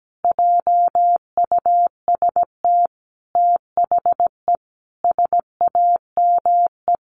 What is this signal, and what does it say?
Morse code, heard as a keyed tone, about 17 wpm, 700 Hz. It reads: JUST THE SAME
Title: Sentences from Top 800 Words 17wpm